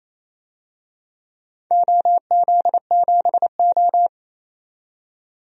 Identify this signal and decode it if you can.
Morse code, heard as a keyed tone, about 28 wpm, 700 Hz. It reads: OZ7O
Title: DXCC-100 Call Signs - 4 Characters - 28wpm